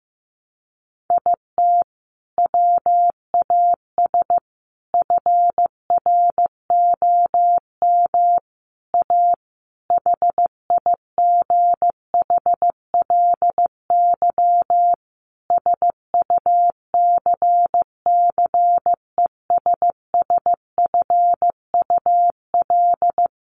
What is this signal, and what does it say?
Morse code, heard as a keyed tone, about 15 wpm, 700 Hz. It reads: IT WAS FROM A HIGHLY SUCCESSFUL
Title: Sentences from Top 1100 Words 15wpm